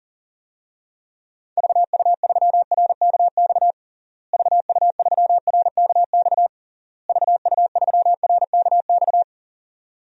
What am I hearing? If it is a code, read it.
Morse code, heard as a keyed tone, about 37 wpm, 700 Hz. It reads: VU3RKX VU3RKX VU3RKX
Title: DXCC-100 Call Signs - 6 Characters - Repeated 3x - 40wpm